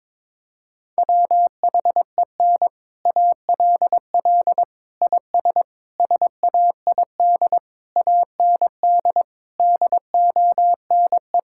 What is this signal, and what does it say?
Morse code, heard as a keyed tone, about 22 wpm, 700 Hz. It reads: WHEN ALL IS SAID AND DONE
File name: practice / 22wpm / Sets of 6 Words - 22wpm